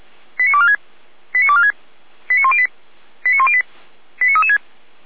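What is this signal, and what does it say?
Five-tone selective call (ZVEI), heard as quick runs of five tones, tones of about 70 ms. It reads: 89246 89246 89108 89108 89336
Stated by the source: ZVEI five-tone call received by SDR.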